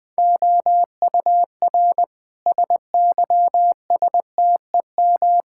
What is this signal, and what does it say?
Morse code, heard as a keyed tone, about 21 wpm, 700 Hz. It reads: OUR SYSTEM
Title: Sentences from Top 1000 Words - Review Part 6 - 20wpm